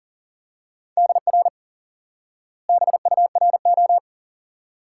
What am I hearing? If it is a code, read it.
Morse code, heard as a keyed tone, about 40 wpm, 700 Hz. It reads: DR BURO
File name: IQE Course 3 - Lesson 89 - YAGI - Two Elements - 40wpm